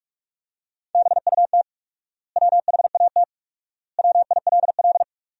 Morse code, heard as a keyed tone, about 46 wpm, 700 Hz. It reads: BUT WHAT WILL
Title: 4x Word-Spacing - Sentences from Top 100 Words - 45wpm